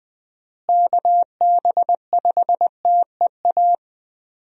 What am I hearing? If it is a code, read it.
Morse code, heard as a keyed tone, about 20 wpm, 700 Hz. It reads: KB5TEA
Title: USA Call Signs - 6 Characters - Repeated 3x - 20wpm